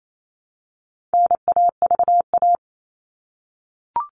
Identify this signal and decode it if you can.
Morse code, heard as a keyed tone, about 28 wpm, 700 Hz. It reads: NAVA
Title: All Cut Numbers - 4-Digit 28wpm